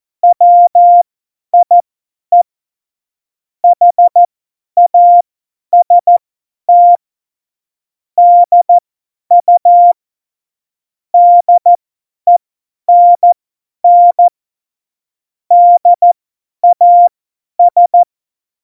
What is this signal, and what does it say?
Morse code, heard as a keyed tone, about 14 wpm, spaced as for 10 wpm, 700 Hz. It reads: WIE HAST DU DENN DAS
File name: Morsekurs / Witze / Witz 05. Januar 2021, 50 F70